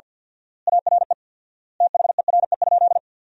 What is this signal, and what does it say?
Morse code, heard as a keyed tone, about 50 wpm, 700 Hz. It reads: ARE THERE?
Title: 4x Word-Spacing - Sentences from Top 100 Words - 50wpm